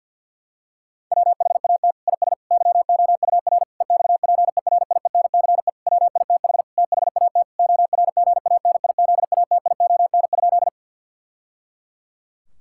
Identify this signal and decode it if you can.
Morse code, heard as a keyed tone, about 51 wpm, 700 Hz. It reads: WHAT IS YOUR EXPERIENCE WITH THAT ORGANIZATION?
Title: Sentences from Top 1000 Words - Review Part 5 - 50wpm